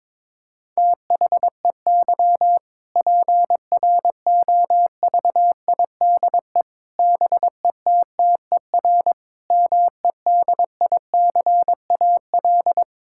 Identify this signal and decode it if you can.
Morse code, heard as a keyed tone, about 22 wpm, 700 Hz. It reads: THEY PROVIDE BETTER MEDICAL
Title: Sentences from Top 1000 Words - Review Part 5 - 22wpm